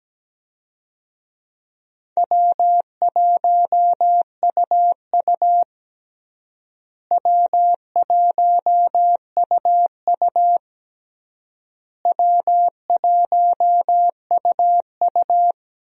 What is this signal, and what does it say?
Morse code, heard as a keyed tone, about 17 wpm, 700 Hz. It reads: W1UU W1UU W1UU
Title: USA Call Signs - 4 Characters - Repeated 3x - 17wpm